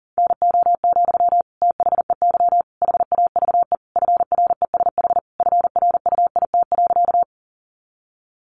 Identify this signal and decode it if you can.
Morse code, heard as a keyed tone, about 40 wpm, 700 Hz. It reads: NO, THEY HAVE FRESH FRUIT.
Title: Sentences from Top 1300 Words 40wpm